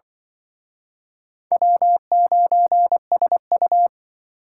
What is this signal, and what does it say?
Morse code, heard as a keyed tone, about 24 wpm, 700 Hz. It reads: W9SU
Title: Speed Racing - USA Call Signs 20wpm